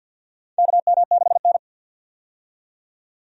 Morse code, heard as a keyed tone, about 50 wpm, 700 Hz. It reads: KM6N
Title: USA Call Signs - 4 Characters - 50wpm